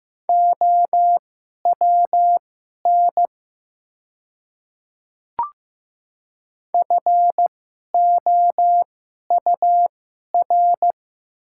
Morse code, heard as a keyed tone, about 15 wpm, spaced as for 11 wpm, 690 Hz. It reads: OWN FOUR
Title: 2x Character Spacing - Top 300 Words - 15wpm